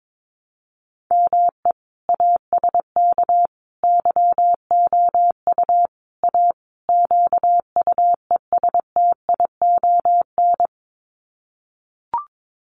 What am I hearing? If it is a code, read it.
Morse code, heard as a keyed tone, about 22 wpm, 700 Hz. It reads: ME ASK YOU A QUESTION
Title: Sets of 5 Words Encore2 22wpm